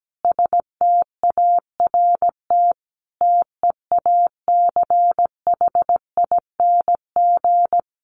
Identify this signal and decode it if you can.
Morse code, heard as a keyed tone, about 17 wpm, 700 Hz. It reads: START TEACHING
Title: Sentences from Top 1100 Words 17wpm